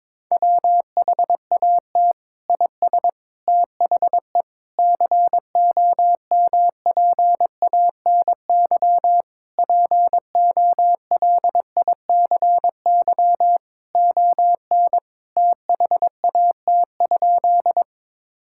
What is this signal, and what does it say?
Morse code, heard as a keyed tone, about 22 wpm, 700 Hz. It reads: WHAT IS THE COMPANY POLICY ON THAT?
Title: Sentences from Top 1000 Words - Review Part 5 - 22wpm